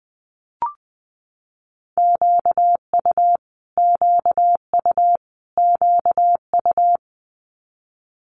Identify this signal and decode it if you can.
Morse code, heard as a keyed tone, about 20 wpm, 700 Hz. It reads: QU QU QU
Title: Multi-letter Phonemes Repeated 3x 20wpm